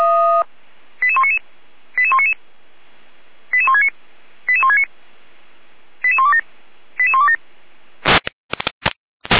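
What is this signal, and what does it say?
Five-tone selective call (ZVEI), heard as quick runs of five tones, tones of about 70 ms. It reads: 88199 88199 88169 88169 88127 88127
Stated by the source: ZVEI five-tone selective call received by SDR on channel 455, recorded by Feuerwehr Orlamünde.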